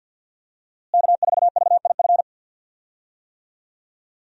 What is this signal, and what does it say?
Morse code, heard as a keyed tone, about 48 wpm, 700 Hz. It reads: K4VIF
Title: USA Call Signs - 5 Characters - Repeated 2x - 50wpm